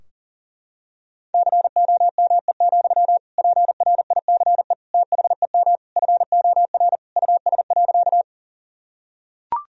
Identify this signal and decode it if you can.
Morse code, heard as a keyed tone, about 40 wpm, 700 Hz. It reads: COME, PRICE THEM FOR US.
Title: Sentences from Top 1000 Words - Review Part 4 - 40wpm